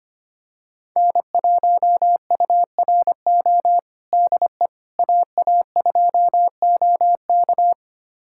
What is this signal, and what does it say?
Morse code, heard as a keyed tone, about 25 wpm, 700 Hz. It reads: N1URO DE AA2OK
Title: USA Callsign DE Callsign - 5 Characters - 25wpm